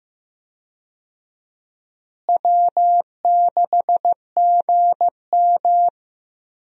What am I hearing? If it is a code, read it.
Morse code, heard as a keyed tone, about 15 wpm, 700 Hz. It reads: W6GM
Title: USA Call Signs - 4 Characters - Repeated 2x - 15wpm